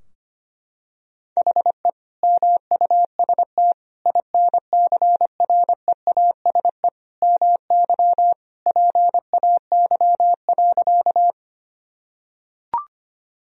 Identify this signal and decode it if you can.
Morse code, heard as a keyed tone, about 25 wpm, 700 Hz. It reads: HE MUST INCREASE MY PAY.
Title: Sentences from Top 1000 Words - Review Part 4 - 25wpm